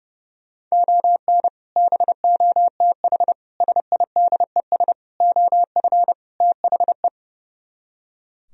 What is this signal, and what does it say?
Morse code, heard as a keyed tone, about 38 wpm, 700 Hz. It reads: ON BOTH SIDES OF THE